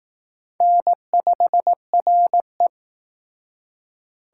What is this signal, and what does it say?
Morse code, heard as a keyed tone, about 18 wpm, 700 Hz. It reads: N5RE
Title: Speed Racing - USA 1x2 Call Signs 15wpm